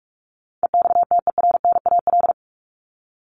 Morse code, heard as a keyed tone, about 44 wpm, 700 Hz. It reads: EXTERNAL
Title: Top 1751-2000 Words 45wpm